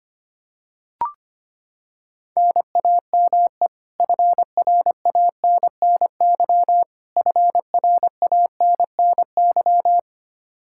Morse code, heard as a keyed tone, about 25 wpm, 700 Hz. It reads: NAME FRANNY FRANNY
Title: Practice QSO - Exchange 1 - Beginning - 25wpm